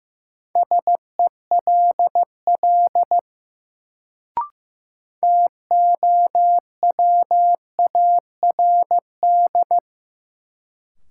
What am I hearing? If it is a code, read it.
Morse code, heard as a keyed tone, about 15 wpm, 700 Hz. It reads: SELL TOWARD
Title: Sentences from Top 1000 Words - Review Part 6 - New Word Drill - 15wpm